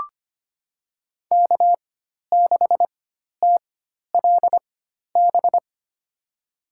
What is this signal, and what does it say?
Morse code, heard as a keyed tone, about 25 wpm, spaced as for 12 wpm, 700 Hz. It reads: K6TLB
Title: USA Call Signs - 4x Character Spacing - 25wpm